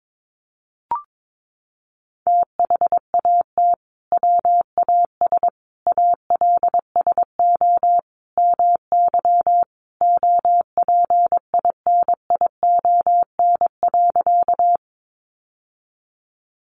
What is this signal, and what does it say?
Morse code, heard as a keyed tone, about 22 wpm, 700 Hz. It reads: THAT WAS ALSO MY OPINION.
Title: Sentences from Top 1200 Words 22wpm